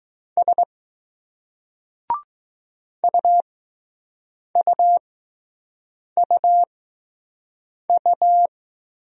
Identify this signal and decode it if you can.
Morse code, heard as a keyed tone, about 20 wpm, spaced as for 6 wpm, 700 Hz. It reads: S UUUU